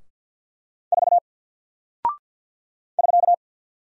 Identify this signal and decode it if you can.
Morse code, heard as a keyed tone, about 48 wpm, 700 Hz. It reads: 4 <SK>
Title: Single Character (Letter:Number:Punctuation:Prosign) 50wpm